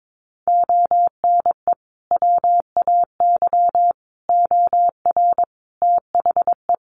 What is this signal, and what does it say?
Morse code, heard as a keyed tone, about 22 wpm, 700 Hz. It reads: ONE WAY OR THE
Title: Sets of 5 Words 22wpm